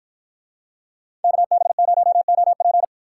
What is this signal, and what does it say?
Morse code, heard as a keyed tone, about 50 wpm, 700 Hz. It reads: KB0OP